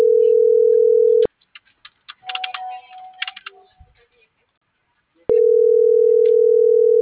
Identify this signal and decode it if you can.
Ringback tone (North America), 440 + 480 Hz.